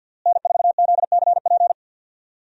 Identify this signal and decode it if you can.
Morse code, heard as a keyed tone, about 50 wpm, 700 Hz. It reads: N4ZCP
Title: USA Call Signs - 5 Characters - Repeated 2x - 50wpm